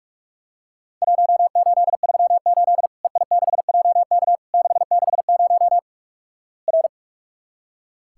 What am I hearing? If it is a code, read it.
Morse code, heard as a keyed tone, about 46 wpm, 700 Hz. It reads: 1838 EI6JK 660 R
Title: Worked All Europe DX Contest - QTC Exchange - 45wpm